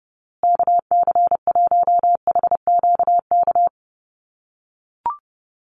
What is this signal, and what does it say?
Morse code, heard as a keyed tone, about 29 wpm, 700 Hz. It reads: KC1HQK
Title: USA Call Signs Repeated 3x 30wpm